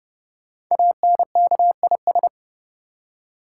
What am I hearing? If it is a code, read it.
Morse code, heard as a keyed tone, about 30 wpm, 700 Hz. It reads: ANKIS
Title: Code Groups - 5 Letters - 30wpm